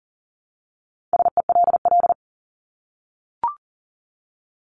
Morse code, heard as a keyed tone, about 38 wpm, 700 Hz. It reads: SELL
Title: Top 1001-1250 Words 40wpm